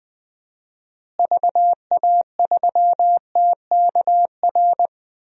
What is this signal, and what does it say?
Morse code, heard as a keyed tone, about 20 wpm, 700 Hz. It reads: VA3TKR
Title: DXCC-100 Call Signs - 6 Characters - Repeated 2x - 20wpm